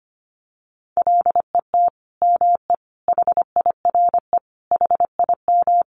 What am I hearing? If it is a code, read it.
Morse code, heard as a keyed tone, about 25 wpm, 700 Hz. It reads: LET ME HIRE HIM
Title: Sentences from Top 1400 Words 25wpm